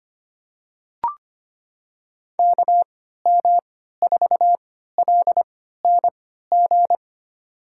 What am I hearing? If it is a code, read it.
Morse code, heard as a keyed tone, about 25 wpm, spaced as for 14 wpm, 700 Hz. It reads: KM4LNG
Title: USA Call Signs - 6 Characters - 3x Character Spacing - 25wpm